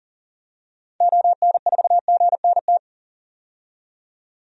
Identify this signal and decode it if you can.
Morse code, heard as a keyed tone, about 40 wpm, 700 Hz. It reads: ON4GNT
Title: DXCC-100 Call Signs - 6 Characters - 40wpm